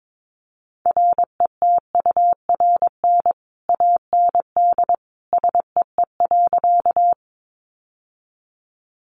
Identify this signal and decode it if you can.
Morse code, heard as a keyed tone, about 22 wpm, 700 Hz. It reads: RETURN AND SEE.
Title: Sentences from Top 800 Words 22wpm